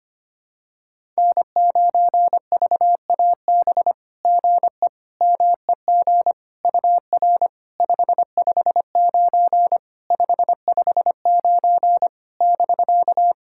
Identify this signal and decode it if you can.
Morse code, heard as a keyed tone, about 25 wpm, 690 Hz. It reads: N9VAB GE MEG UR 559 559 <BK>
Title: SOTA QSO Practice 25wpm